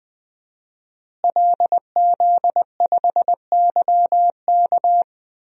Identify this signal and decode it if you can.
Morse code, heard as a keyed tone, about 20 wpm, 700 Hz. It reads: LZ5YK
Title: DXCC-100 Call Signs - 5 Characters - Repeated 2x - 20wpm